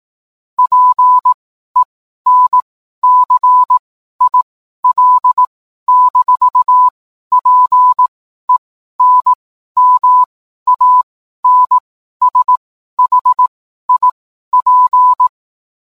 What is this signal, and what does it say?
Morse code, heard as a keyed tone, about 18 wpm, spaced as for 12 wpm, 1000 Hz. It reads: PENCIL-PENMANSHIP
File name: cwblog / cw mp3 directory / mem morse 12